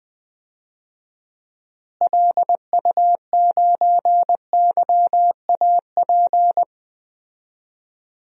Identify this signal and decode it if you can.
Morse code, heard as a keyed tone, about 21 wpm, 700 Hz. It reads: LU9YAP
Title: DXCC-100 Call Signs - 6 Characters - Repeated 3x - 20wpm